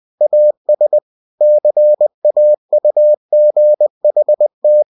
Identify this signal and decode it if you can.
Morse code, heard as a keyed tone, about 20 wpm, 600 Hz. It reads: AS CAUGHT
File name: cw-books / winnie-the-pooh / winnie-20wpm-page-0035